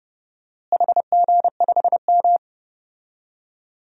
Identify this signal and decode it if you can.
Morse code, heard as a keyed tone, about 30 wpm, 700 Hz. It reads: HG5M
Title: DXCC-100 Call Signs - 4 Characters - Repeated 2x - 30wpm